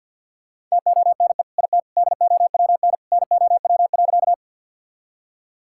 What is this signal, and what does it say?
Morse code, heard as a keyed tone, about 50 wpm, 700 Hz. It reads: TONE IT DOWN NOW.
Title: Sentences from Top 1600 Words 50wpm